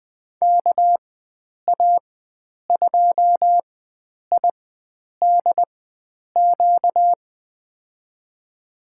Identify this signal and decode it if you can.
Morse code, heard as a keyed tone, about 20 wpm, spaced as for 9 wpm, 700 Hz. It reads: KA2IDQ
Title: USA Call Signs - 6 Characters - 4x Character Spacing - 20wpm